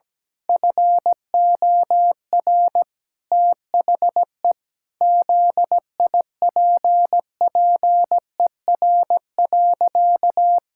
Morse code, heard as a keyed tone, about 17 wpm, 700 Hz. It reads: FOR THE ZIPPER.
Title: Pangram Challenge v2 - 17wpm